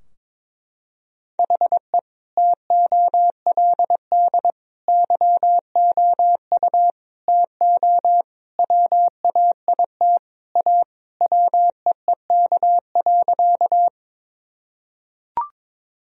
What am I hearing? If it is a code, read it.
Morse code, heard as a keyed tone, about 22 wpm, 700 Hz. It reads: HE TOLD YOU TO WAIT A WEEK.
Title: Sentences from Top 1000 Words - Review Part 2 - 22wpm